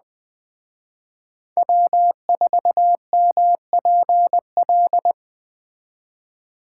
Morse code, heard as a keyed tone, about 20 wpm, 700 Hz. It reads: W4MPL
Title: Speed Racing - USA 1x3 Call Signs 17wpm